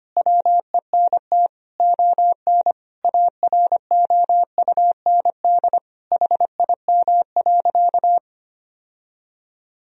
Morse code, heard as a keyed tone, about 25 wpm, 700 Hz. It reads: WENT ON AROUND HIM.
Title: Sentences from Top 1000 Words - Review Part 3 - 25wpm